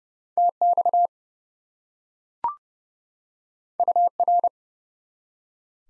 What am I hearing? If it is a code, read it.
Morse code, heard as a keyed tone, about 30 wpm, 700 Hz. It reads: TX UR
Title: Top 90 Abbr 30WPM